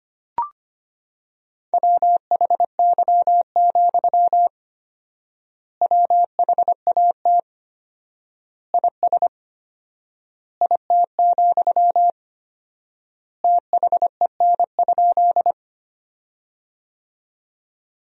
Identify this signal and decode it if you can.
Morse code, heard as a keyed tone, about 25 wpm, 700 Hz. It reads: WHY, WHAT IS IT, THEN?
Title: 4x Word-Spacing - Sentences from Top 200 Words - 25wpm